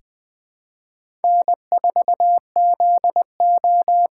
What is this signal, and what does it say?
Morse code, heard as a keyed tone, about 20 wpm, 700 Hz. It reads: N4ZO
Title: USA Call Signs - 6x Character Spacing - 20wpm